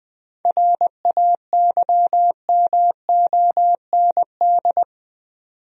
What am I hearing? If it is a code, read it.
Morse code, heard as a keyed tone, about 20 wpm, 700 Hz. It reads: RAYMOND
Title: USA Common Names - 20wpm